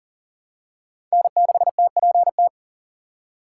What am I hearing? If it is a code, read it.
Morse code, heard as a keyed tone, about 40 wpm, 700 Hz. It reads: N6TPT
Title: USA Call Signs - 5 Characters - 2x Character Spacing - 40wpm